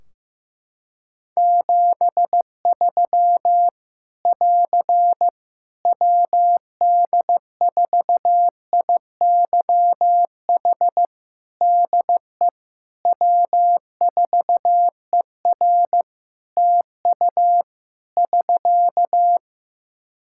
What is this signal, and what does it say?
Morse code, heard as a keyed tone, about 15 wpm, 700 Hz. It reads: 73 <AR> WD4IYH DE W4ER TU <SK>
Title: Practice QSO - Exchange 4 - Ending - 15wpm